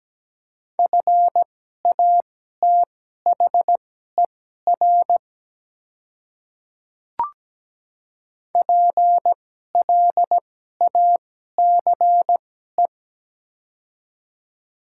Morse code, heard as a keyed tone, about 17 wpm, spaced as for 12 wpm, 700 Hz. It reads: FATHER PLACE
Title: 2x Character Spacing - Top 200 Words - 17wpm